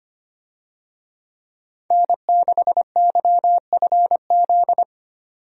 Morse code, heard as a keyed tone, about 25 wpm, 700 Hz. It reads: N6YFZ
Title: USA Call Signs - 5 Characters - 25wpm